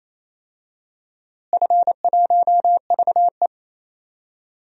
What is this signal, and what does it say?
Morse code, heard as a keyed tone, about 27 wpm, 700 Hz. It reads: F1VE
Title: DXCC-100 Call Signs - 4 Characters - Repeated 3x - 28wpm